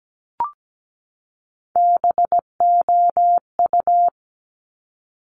Morse code, heard as a keyed tone, about 17 wpm, 700 Hz. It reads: BOU
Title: Top Three Letter Combinations in Words 17wpm